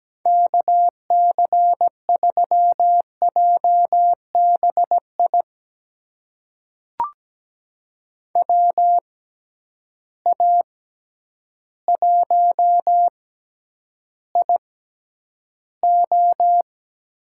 Morse code, heard as a keyed tone, about 17 wpm, spaced as for 9 wpm, 700 Hz. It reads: KC3JBI WA1IO
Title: USA Call Signs - 6 Characters - 6x Character Spacing - 17wpm